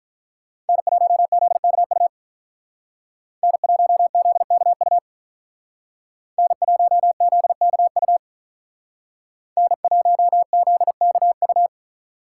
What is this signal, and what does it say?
Morse code, heard as a keyed tone, about 43 wpm, 700 Hz. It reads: N1ZKU N1ZKU N1ZKU N1ZKU
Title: Speed Racing - USA 1x3 Call Signs 35wpm